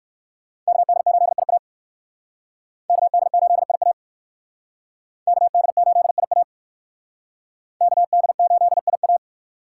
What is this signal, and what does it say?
Morse code, heard as a keyed tone, about 57 wpm, 700 Hz. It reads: KD8IA KD8IA KD8IA KD8IA